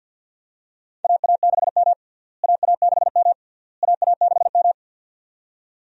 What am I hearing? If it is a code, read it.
Morse code, heard as a keyed tone, about 52 wpm, 700 Hz. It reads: AA6M AA6M AA6M
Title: USA Call Signs - 4 Characters - Repeated 3x - 50wpm